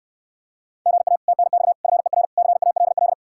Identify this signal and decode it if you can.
Morse code, heard as a keyed tone, about 67 wpm, 700 Hz. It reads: GA TNX FER CALL